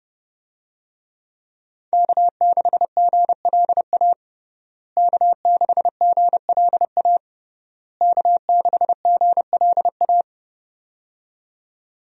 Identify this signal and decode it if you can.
Morse code, heard as a keyed tone, about 30 wpm, 700 Hz. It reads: K6GLA K6GLA K6GLA
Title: USA Call Signs - 5 Characters - Repeated 3x - 30wpm